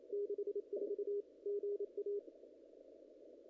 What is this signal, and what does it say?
Morse code, heard as a keyed tone, about 27 wpm, 440 Hz. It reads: 64 GA
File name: contests / ss-cw-k5tr-op-n5rz -0100